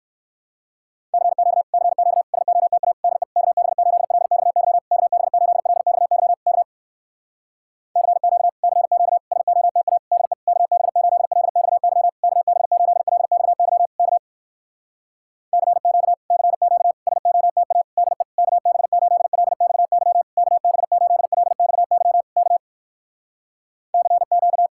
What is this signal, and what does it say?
Morse code, heard as a keyed tone, about 59 wpm, 700 Hz. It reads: CQ CQ SOTA DE KB8LXY KB8LXY K CQ CQ SOTA DE KB8LXY KB8LXY K CQ CQ SOTA DE KB8LXY KB8LXY K CQ